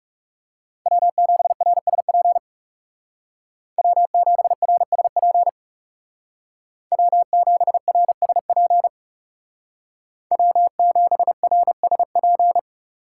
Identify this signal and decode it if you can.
Morse code, heard as a keyed tone, about 38 wpm, 700 Hz. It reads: W7RSP W7RSP W7RSP W7RSP